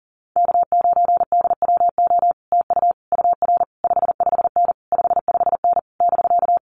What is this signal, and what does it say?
Morse code, heard as a keyed tone, about 40 wpm, 700 Hz. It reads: K9DWO TU UR 55N 55N <BK>
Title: Common QSO Phrases Encore 40wpm